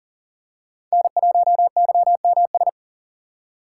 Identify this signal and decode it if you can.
Morse code, heard as a keyed tone, about 40 wpm, 700 Hz. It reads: N1YMS
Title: USA Call Signs - 5 Characters - 4x Character Spacing - 40wpm